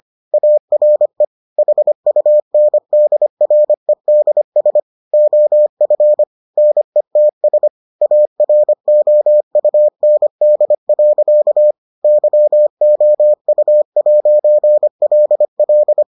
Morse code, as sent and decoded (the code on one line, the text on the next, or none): .- .-. / .... ..- -. -.. .-. -.. ... / --- ..-. / -. - ... / .- .-. --- ..- -. -.. .-.-.- / -.-- --- ..- .----. .-.. .-..
ARE HUNDREDS OF NETS AROUND. YOU'LL